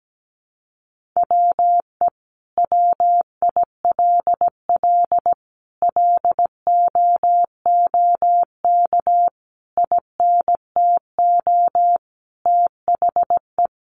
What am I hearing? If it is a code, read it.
Morse code, heard as a keyed tone, about 17 wpm, 700 Hz. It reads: WE WILL LOOK INTO THE
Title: Sentences from Top 1400 Words 17wpm